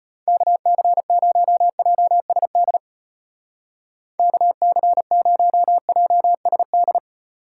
Morse code, heard as a keyed tone, about 31 wpm, 700 Hz. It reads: KC0JSD KC0JSD
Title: Speed Racing - USA Call Signs 25wpm